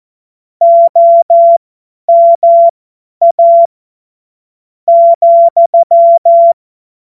Morse code, heard as a keyed tone, about 14 wpm, spaced as for 8 wpm, 680 Hz. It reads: OMA,
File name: Morsekurs / Witze / Witz 12. Oktober 2021, 50 F70